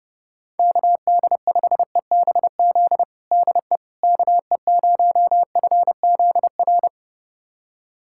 Morse code, heard as a keyed tone, about 32 wpm, 700 Hz. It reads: KD5EBZ DE KE0FZR
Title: USA Callsign DE Callsign - 6 Characters - 30wpm